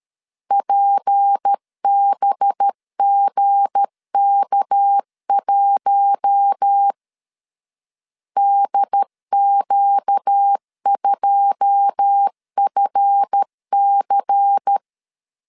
Morse code, heard as a keyed tone, about 13 wpm, 800 Hz. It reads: PBGK1 DQ2FC
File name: LEZIONI / 9 2 12wpm 2301